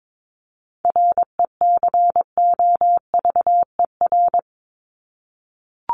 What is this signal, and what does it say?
Morse code, heard as a keyed tone, about 22 wpm, 700 Hz. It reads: RECOVER
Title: Top 1751-2000 Words 22wpm